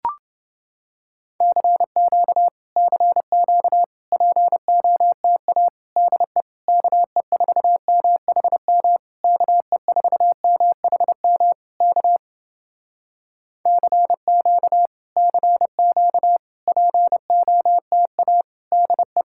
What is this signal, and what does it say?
Morse code, heard as a keyed tone, about 30 wpm, 700 Hz. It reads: CQ CQ POTA DE KE4MHM KE4MHM K CQ CQ POTA DE